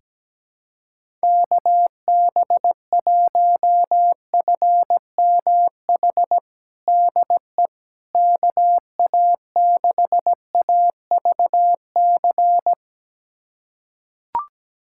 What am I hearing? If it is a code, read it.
Morse code, heard as a keyed tone, about 17 wpm, 700 Hz. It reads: KB1FMH DE KA6AVC
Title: USA Callsign DE Callsign - 6 Characters - 17wpm